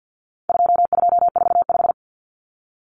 Morse code, heard as a keyed tone, about 50 wpm, 700 Hz. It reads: 2245
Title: Numbers - 4-Digit 50wpm